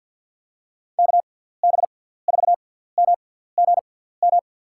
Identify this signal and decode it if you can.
Morse code, heard as a keyed tone, about 52 wpm, spaced as for 17 wpm, 700 Hz. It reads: KB4MGM
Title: USA Call Signs - 6 Characters - 6x Character Spacing - 50wpm